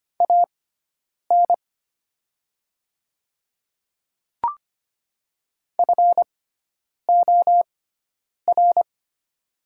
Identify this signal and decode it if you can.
Morse code, heard as a keyed tone, about 25 wpm, spaced as for 9 wpm, 690 Hz. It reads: AN FOR
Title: 6x Character Spacing - Top 100 Words - 25wpm